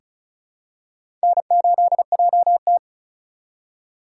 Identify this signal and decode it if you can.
Morse code, heard as a keyed tone, about 34 wpm, 700 Hz. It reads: N8JT
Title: USA Call Signs - 4 Characters - 3x Character Spacing - 35wpm